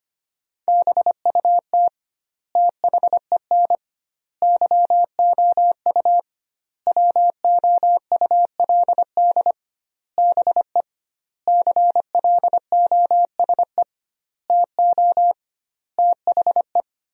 Morse code, heard as a keyed tone, about 25 wpm, 700 Hz. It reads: BUT THEN YOU WOULD BE CLOSE TO THE
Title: 2x Word-Spacing - Sentences from Top 300 Words - 25wpm